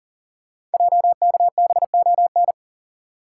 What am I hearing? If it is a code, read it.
Morse code, heard as a keyed tone, about 38 wpm, 700 Hz. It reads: JKBON
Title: Code Groups - 5 Letter:Numbers - 40wpm